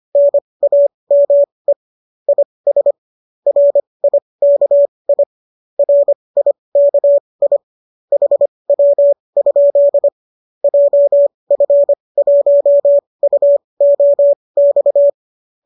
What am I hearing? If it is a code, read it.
Morse code, heard as a keyed tone, about 25 wpm, spaced as for 20 wpm, 570 Hz. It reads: NAME IS RIKI RIKI HW? JF1UOX